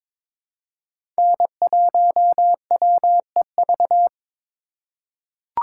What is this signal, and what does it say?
Morse code, heard as a keyed tone, about 22 wpm, 700 Hz. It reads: N1WEV
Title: USA Call Signs - 5 Characters - 2x Character Spacing - 22wpm